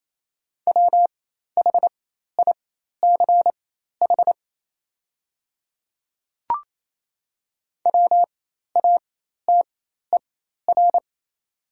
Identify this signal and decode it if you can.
Morse code, heard as a keyed tone, about 28 wpm, spaced as for 13 wpm, 700 Hz. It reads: WHICH WATER
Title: 4x Character Spacing - Top 200 Words - 28wpm